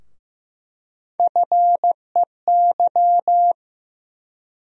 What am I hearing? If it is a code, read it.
Morse code, heard as a keyed tone, about 15 wpm, 700 Hz. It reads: FEY
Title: USA Call Sign Postfixes - 3 Letters 15wpm